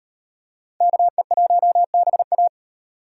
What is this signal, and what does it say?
Morse code, heard as a keyed tone, about 37 wpm, 700 Hz. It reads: KE1BA